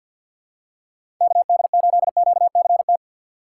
Morse code, heard as a keyed tone, about 49 wpm, 700 Hz. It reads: KD8QCT
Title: USA Call Signs - 4x Character Spacing - 50wpm